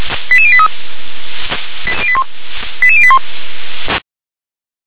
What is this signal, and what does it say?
Five-tone selective call (ZVEI), heard as quick runs of five tones, tones of about 70 ms. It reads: 88083 88081